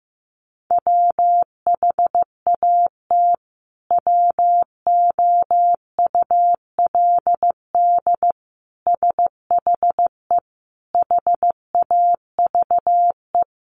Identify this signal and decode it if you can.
Morse code, heard as a keyed tone, about 15 wpm, 700 Hz. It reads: WHAT WOULD SHE HAVE
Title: Sentences from Top 100 Words 15wpm